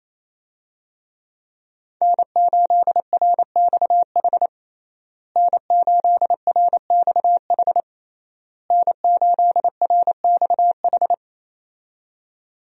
Morse code, heard as a keyed tone, about 28 wpm, 700 Hz. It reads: N8RXH N8RXH N8RXH
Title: USA Call Signs - 5 Characters - Repeated 3x - 28wpm